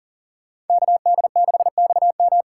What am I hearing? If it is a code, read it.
Morse code, heard as a keyed tone, about 52 wpm, 700 Hz. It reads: KD6XM